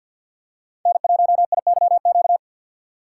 Morse code, heard as a keyed tone, about 52 wpm, 700 Hz. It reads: N1IYQ
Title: USA Call Signs - 6x Character Spacing - 50wpm